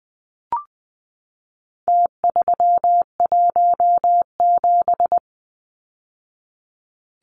Morse code, heard as a keyed tone, about 20 wpm, 700 Hz. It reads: T317
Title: Numbers - Contest Style - 4-Digit 20wpm